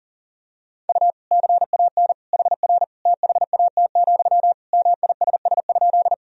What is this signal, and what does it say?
Morse code, heard as a keyed tone, about 40 wpm, 700 Hz. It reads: U CAN HR THAT, MISS?
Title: Sentences from Top 500 Words and Abbreviations 40wpm